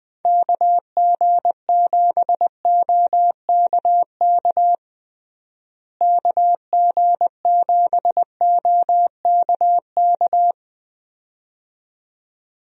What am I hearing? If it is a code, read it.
Morse code, heard as a keyed tone, about 21 wpm, 700 Hz. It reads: KG7OKK KG7OKK
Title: USA Call Signs - 6 Characters - Repeated 2x - 20wpm